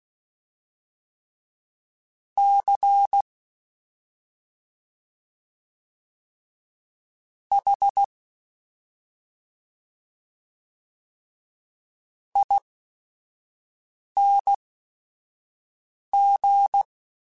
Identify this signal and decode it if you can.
Morse code, heard as a keyed tone, about 16 wpm, spaced as for 5 wpm, 800 Hz. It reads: C H ING
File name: Special INGs / Special INGs-mp3s / SlowToFaster ing @5